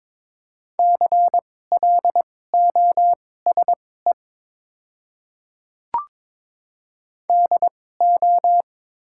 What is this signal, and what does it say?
Morse code, heard as a keyed tone, about 22 wpm, spaced as for 16 wpm, 700 Hz. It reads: CLOSE DO
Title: 2x Character Spacing - Top 300 Words - 22wpm